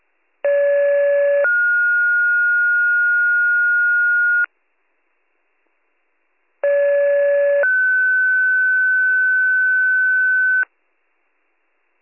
Two-tone page, heard as a one-second tone, then a three-second tone, about 570 then 1430 Hz, about 570 then 1500 Hz. It is heard from 0.4 to 10.6 s.